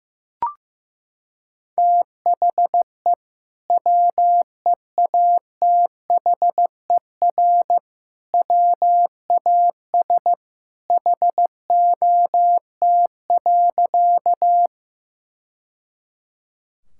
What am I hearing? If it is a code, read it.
Morse code, heard as a keyed tone, about 14 wpm, 700 Hz. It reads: THE WEATHER WAS HOT.
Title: Sentences from Top 1000 Words - Review Part 5 - 15wpm